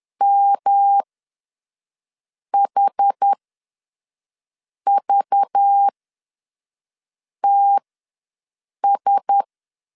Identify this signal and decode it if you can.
Morse code, heard as a keyed tone, about 10 wpm, spaced as for 5 wpm, 800 Hz. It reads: MHVTS
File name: LEZIONI / 5 LBVC / 5 LBV ws 10wpm 3645